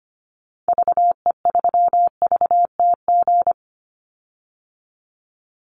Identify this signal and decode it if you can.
Morse code, heard as a keyed tone, about 25 wpm, 700 Hz. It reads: VE3VTG
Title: Call Signs - 25wpm